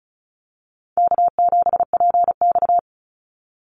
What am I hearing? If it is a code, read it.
Morse code, heard as a keyed tone, about 34 wpm, 700 Hz. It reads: K7PX
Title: NA SOTA Chaser Callsigns 35wpm